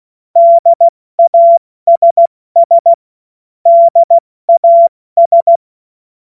Morse code, heard as a keyed tone, about 16 wpm, 680 Hz. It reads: DASS DAS
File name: Morsekurs / Witze / Witz 12 März 2024 70 F80